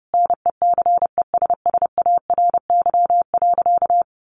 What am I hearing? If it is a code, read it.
Morse code, heard as a keyed tone, about 30 wpm, 700 Hz. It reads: NECESSARY.
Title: Sentences from Top 800 Words 30wpm